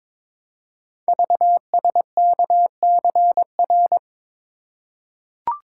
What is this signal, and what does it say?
Morse code, heard as a keyed tone, about 22 wpm, 700 Hz. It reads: VSKCR
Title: Code Groups - 5 Letters - 22wpm